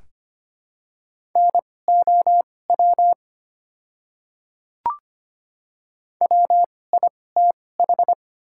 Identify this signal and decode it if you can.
Morse code, heard as a keyed tone, about 25 wpm, spaced as for 18 wpm, 700 Hz. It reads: NOW WITH
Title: 2x Character Spacing - Top 100 Words - 25wpm